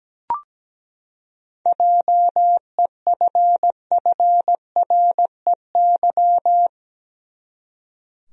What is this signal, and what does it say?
Morse code, heard as a keyed tone, about 17 wpm, 700 Hz. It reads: JEFFREY
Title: USA Common Names - 17wpm